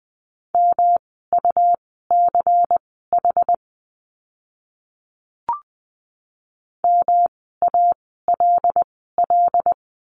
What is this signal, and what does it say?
Morse code, heard as a keyed tone, about 20 wpm, spaced as for 15 wpm, 700 Hz. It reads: MUCH MALL
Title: 108 Lesson 18f - Letter m - Words 20wpm